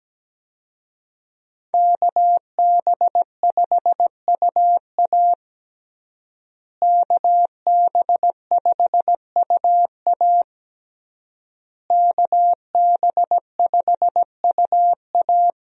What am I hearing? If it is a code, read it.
Morse code, heard as a keyed tone, about 17 wpm, 700 Hz. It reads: KB5UA KB5UA KB5UA
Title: USA Call Signs - 5 Characters - Repeated 3x - 17wpm